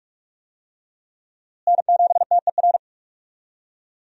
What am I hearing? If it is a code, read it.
Morse code, heard as a keyed tone, about 46 wpm, 700 Hz. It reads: N7TER
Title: USA Call Signs - 5 Characters - Repeated 3x - 45wpm